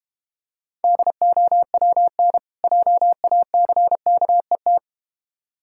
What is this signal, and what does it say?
Morse code, heard as a keyed tone, about 32 wpm, 700 Hz. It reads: DOWN JACKET